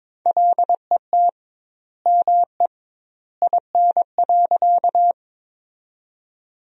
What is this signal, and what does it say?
Morse code, heard as a keyed tone, about 21 wpm, 700 Hz. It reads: LET ME IN.
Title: 2x Word-Spacing - Sentences from Top 300 Words - 22wpm